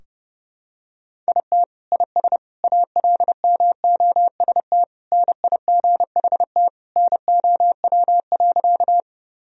Morse code, heard as a keyed tone, about 30 wpm, 700 Hz. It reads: IT IS ALMOST NIGHT NOW.
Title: Sentences from Top 1000 Words - Review Part 4 - 30wpm